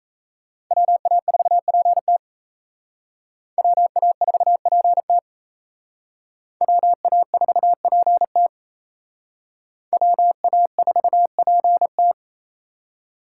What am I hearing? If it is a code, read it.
Morse code, heard as a keyed tone, about 35 wpm, 700 Hz. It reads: WA4PT WA4PT WA4PT WA4PT
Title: Speed Racing - USA 2x2 Call Signs 28wpm